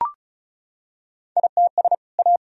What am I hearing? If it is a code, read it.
Morse code, heard as a keyed tone, about 34 wpm, 700 Hz. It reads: ITS A
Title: Sentences from Top 1000 Words - Review Part 1 - 35wpm